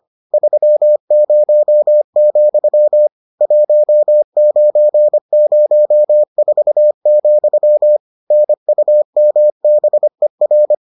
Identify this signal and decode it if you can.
Morse code, heard as a keyed tone, about 25 wpm, 600 Hz. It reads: 30, 1904, NUMBER